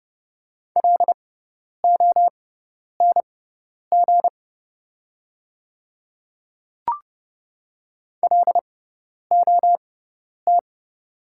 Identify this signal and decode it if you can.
Morse code, heard as a keyed tone, about 31 wpm, spaced as for 10 wpm, 690 Hz. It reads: LONG LOT
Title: 6x Character Spacing - Top 500 Words - 30wpm